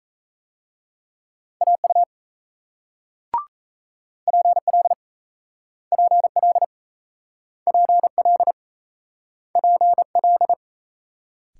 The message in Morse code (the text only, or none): AU PL PL PL PL